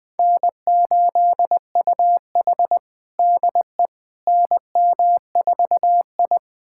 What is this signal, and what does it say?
Morse code, heard as a keyed tone, about 21 wpm, 700 Hz. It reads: N8UH DE NM4I
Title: USA Callsign DE Callsign - 4 Characters - 20wpm